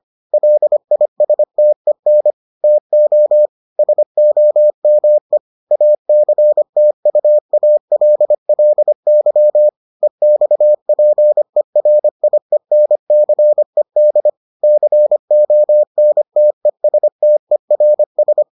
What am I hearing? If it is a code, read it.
Morse code, heard as a keyed tone, about 25 wpm, 600 Hz. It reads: LISTEN TO SOME ACTUALLY EXPERIENCED CONTESTERS